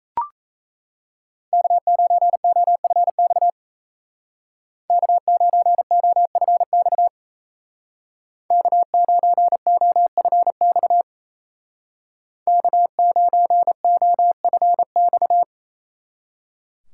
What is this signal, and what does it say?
Morse code, heard as a keyed tone, about 37 wpm, 700 Hz. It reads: K9OFX K9OFX K9OFX K9OFX